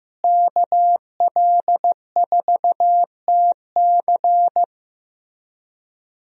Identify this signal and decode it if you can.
Morse code, heard as a keyed tone, about 15 wpm, 700 Hz. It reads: KL4TC
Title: USA Call Signs - 5 Characters - Repeated 3x - 15wpm